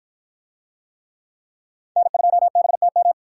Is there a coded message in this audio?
Yes: Morse code from 2.0 to 3.1 s, heard as a keyed tone, about 53 wpm, 700 Hz. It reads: N2BTM